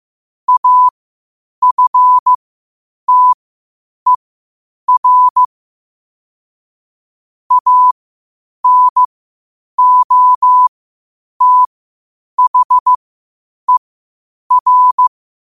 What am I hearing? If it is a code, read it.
Morse code, heard as a keyed tone, about 15 wpm, spaced as for 8 wpm, 1000 Hz. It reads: AFTER ANOTHER
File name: qotdfiles / 8wpm